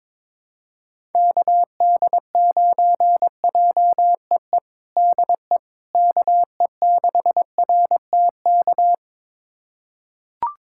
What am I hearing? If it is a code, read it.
Morse code, heard as a keyed tone, about 21 wpm, 700 Hz. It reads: KD9JEE DE KE6RTK
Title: USA Callsign DE Callsign - 6 Characters - 22wpm